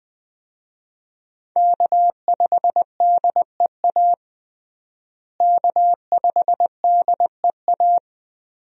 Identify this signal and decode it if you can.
Morse code, heard as a keyed tone, about 20 wpm, 700 Hz. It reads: K5DEA K5DEA
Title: USA Call Signs - 5 Characters - Repeated 2x - 20wpm